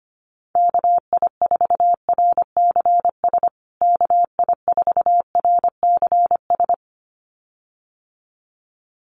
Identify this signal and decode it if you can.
Morse code, heard as a keyed tone, about 25 wpm, 700 Hz. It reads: KI4RCS KI4RCS
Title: USA Call Signs Repeated 2x 25wpm